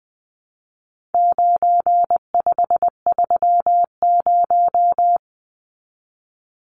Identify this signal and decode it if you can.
Morse code, heard as a keyed tone, about 20 wpm, 700 Hz. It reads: 9530
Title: Numbers - 4-Digit 20wpm